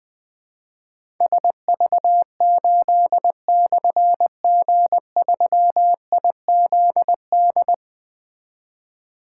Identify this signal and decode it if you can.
Morse code, heard as a keyed tone, about 20 wpm, 700 Hz. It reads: SV8/G3IZD
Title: Difficult Call Signs 20wpm